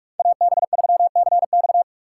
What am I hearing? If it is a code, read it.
Morse code, heard as a keyed tone, about 43 wpm, 700 Hz. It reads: AB3CX
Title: USA Call Signs - 5 Characters - 3x Character Spacing - 45wpm